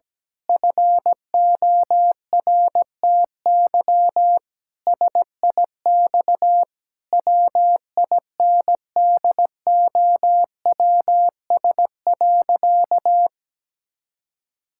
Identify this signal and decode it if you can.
Morse code, heard as a keyed tone, about 17 wpm, 700 Hz. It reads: FORTY SIX WINDOWS.
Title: Pangram Challenge v2 - 17wpm